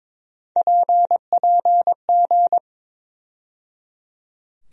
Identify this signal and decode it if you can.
Morse code, heard as a keyed tone, about 22 wpm, 700 Hz. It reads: PPG
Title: USA Call Sign Postfixes - 3 Letters 22wpm